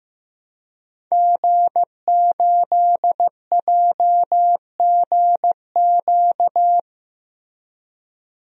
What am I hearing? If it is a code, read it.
Morse code, heard as a keyed tone, about 15 wpm, 700 Hz. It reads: G8JGQ
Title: DXCC-100 Call Signs - 5 Characters - Repeated 2x - 15wpm